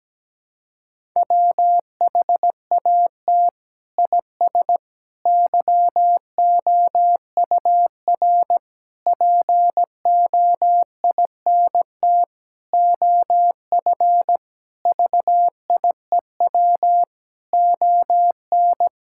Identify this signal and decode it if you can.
Morse code, heard as a keyed tone, about 17 wpm, 700 Hz. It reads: WHAT IS YOUR POINT OF VIEW ON
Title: Sentences from Top 1000 Words - Review Part 6 - 17wpm